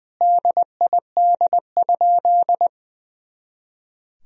Morse code, heard as a keyed tone, about 20 wpm, 700 Hz. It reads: DID?
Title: Sentences from Top 100 Words and Abbreviations 20wpm